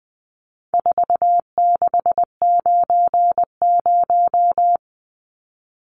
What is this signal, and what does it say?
Morse code, heard as a keyed tone, about 20 wpm, 700 Hz. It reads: 4690
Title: Numbers - 4-Digit 20wpm